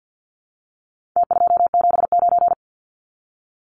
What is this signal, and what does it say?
Morse code, heard as a keyed tone, about 49 wpm, 700 Hz. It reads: T279
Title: Numbers - Contest Style - 4-Digit 50wpm